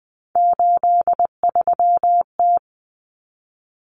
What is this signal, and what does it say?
Morse code, heard as a keyed tone, about 20 wpm, 700 Hz. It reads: 83T
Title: Common Cut Numbers - 3-Digit 20wpm